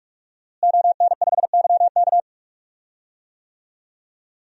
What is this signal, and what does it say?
Morse code, heard as a keyed tone, about 46 wpm, 700 Hz. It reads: ON5YK
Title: DXCC-100 Call Signs - 5 Characters - Repeated 2x - 45wpm